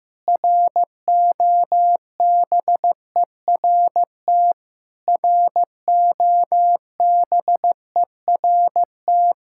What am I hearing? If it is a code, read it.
Morse code, heard as a keyed tone, about 15 wpm, 700 Hz. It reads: ROBERT ROBERT
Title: Practice QSO - Exchange 1 - Beginning - 15wpm